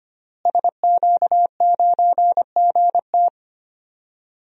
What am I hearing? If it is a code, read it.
Morse code, heard as a keyed tone, about 25 wpm, 700 Hz. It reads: SQ9GT
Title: DXCC-100 Call Signs - 5 Characters - Repeated 3x - 25wpm